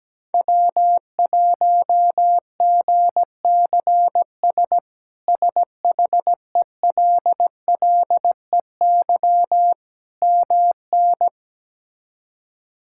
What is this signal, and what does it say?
Morse code, heard as a keyed tone, about 17 wpm, 700 Hz. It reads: W1GCS SHELLEY MN
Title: K1USN SST QSO Practice - 17wpm